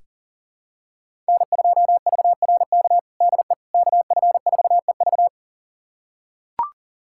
Morse code, heard as a keyed tone, about 37 wpm, 700 Hz. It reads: N2VRK DE KF4EV
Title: USA Callsign DE Callsign - 5 Characters - 40wpm